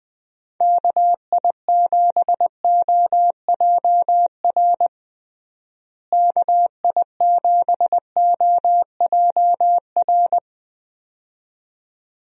Morse code, heard as a keyed tone, about 20 wpm, 700 Hz. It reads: KI7OJR KI7OJR
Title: USA Call Signs - 6 Characters - Repeated 2x - 20wpm